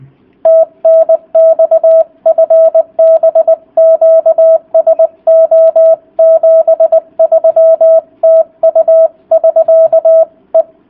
Morse code, heard as a keyed tone, about 20 wpm, 650 Hz. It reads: TNX FB QSO 73 TU <SK> E